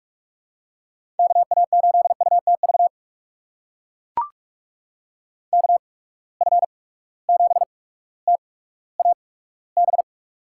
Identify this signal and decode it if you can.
Morse code, heard as a keyed tone, about 44 wpm, spaced as for 19 wpm, 700 Hz. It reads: KA8UTV KF7TAB
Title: USA Call Signs - 6 Characters - 8x Character Spacing - 45wpm